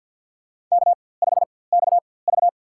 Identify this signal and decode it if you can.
Morse code, heard as a keyed tone, about 48 wpm, spaced as for 23 wpm, 700 Hz. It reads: K5XV
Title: USA Call Signs - 4x Character Spacing - 50wpm